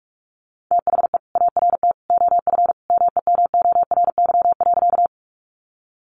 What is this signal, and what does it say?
Morse code, heard as a keyed tone, about 44 wpm, 700 Hz. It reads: THE ART OF MEMORY.
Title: Sentences from Top 1300 Words 45wpm